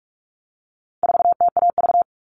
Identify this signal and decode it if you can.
Morse code, heard as a keyed tone, about 44 wpm, 700 Hz. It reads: 4TAV
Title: All Cut Numbers - 4-Digit 45wpm